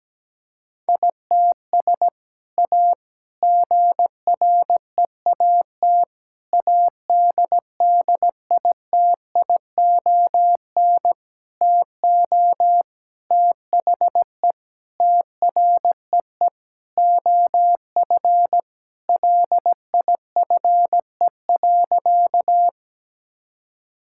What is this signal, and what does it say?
Morse code, heard as a keyed tone, about 17 wpm, 700 Hz. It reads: ITS A GREAT ADDITION TO THE TREE OF LIFE.
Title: Sentences from Top 1000 Words - Review Part 6 - 17wpm